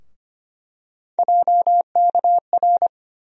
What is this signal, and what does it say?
Morse code, heard as a keyed tone, about 25 wpm, 700 Hz. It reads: JKR
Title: USA Call Sign Postfixes - 3 Letters 25wpm